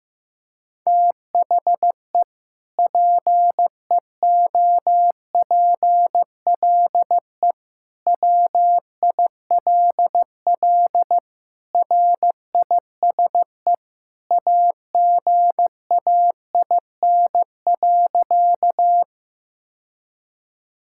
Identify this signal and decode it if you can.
Morse code, heard as a keyed tone, about 15 wpm, 700 Hz. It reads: THE PEOPLE WILL RISE AGAIN.
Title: Sentences from Top 1000 Words - Review Part 6 - 15wpm